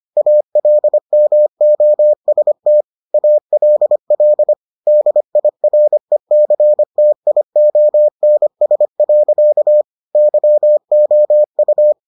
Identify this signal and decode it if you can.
Morse code, heard as a keyed tone, about 25 wpm, 600 Hz. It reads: ALMOST ALL DIRECTIONS. YOU